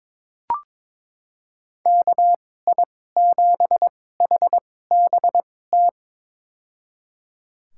Morse code, heard as a keyed tone, about 22 wpm, spaced as for 16 wpm, 700 Hz. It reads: KI7HBT
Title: USA Call Signs - 6 Characters - 2x Character Spacing - 22wpm